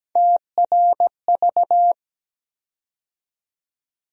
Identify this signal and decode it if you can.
Morse code, heard as a keyed tone, about 17 wpm, 700 Hz. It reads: TRV
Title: USA Call Sign Postfixes - 3 Letters 17wpm